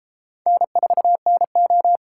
Morse code, heard as a keyed tone, about 33 wpm, 700 Hz. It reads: N4NO